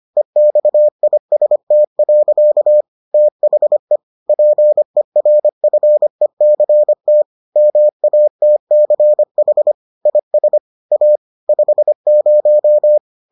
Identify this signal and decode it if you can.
Morse code, heard as a keyed tone, about 25 wpm, 600 Hz. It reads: EXIST. THE PERFECT MATCH IS A 50